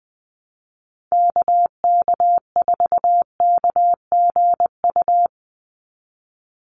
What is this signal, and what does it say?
Morse code, heard as a keyed tone, about 20 wpm, 700 Hz. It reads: KK4KGU
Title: USA Call Signs 20wpm